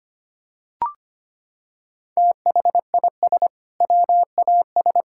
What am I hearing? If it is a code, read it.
Morse code, heard as a keyed tone, about 25 wpm, 700 Hz. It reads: THIS WAS
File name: practice / 25wpm / Sentences with 2-Word Phrases - 25wpm